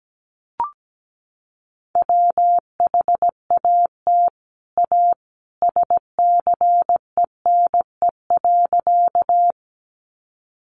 Morse code, heard as a keyed tone, about 17 wpm, 700 Hz. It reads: WHAT A SCENE.
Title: Sentences from Top 1100 Words 17wpm